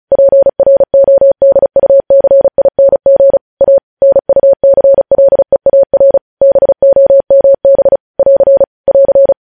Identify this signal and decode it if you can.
Morse code, heard as a keyed tone, about 36 wpm, 550 Hz. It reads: PRODUCING A NUCLEAR BOMB <AR> <AR>
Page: news headlines in morse code at 35 wpm